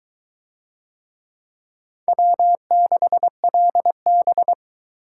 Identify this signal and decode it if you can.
Morse code, heard as a keyed tone, about 21 wpm, 700 Hz. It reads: W6LB